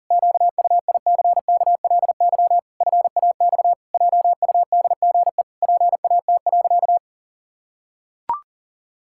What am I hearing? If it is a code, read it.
Morse code, heard as a keyed tone, about 40 wpm, 700 Hz. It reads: QUICKLY FAX JUDGE PAT.
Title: Pangram Challenge v2 - 40wpm